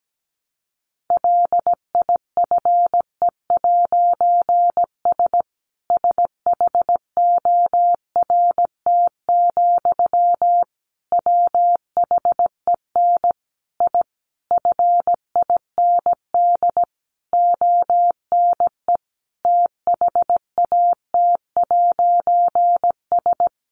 Morse code, heard as a keyed tone, about 17 wpm, 700 Hz. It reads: LIFE'S SHORT, WHEN I FIND ONE THAT'S
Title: A Sound Of Thunder 17wpm